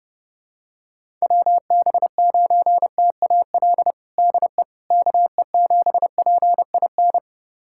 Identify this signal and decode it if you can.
Morse code, heard as a keyed tone, about 30 wpm, 700 Hz. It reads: WB9TAL DE KE7PIN
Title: USA Callsign DE Callsign - 6 Characters - 30wpm